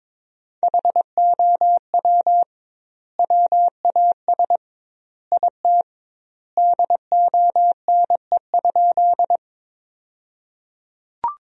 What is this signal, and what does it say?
Morse code, heard as a keyed tone, about 22 wpm, 700 Hz. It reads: HOW WAS IT DONE?
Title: 2x Word-Spacing - Sentences from Top 500 Words - 22wpm